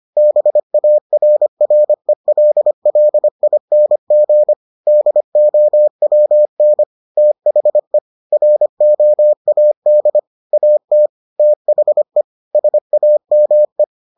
Morse code, heard as a keyed tone, about 25 wpm, 600 Hz. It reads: BARRELLING DOWN THE ROAD AT THE SAME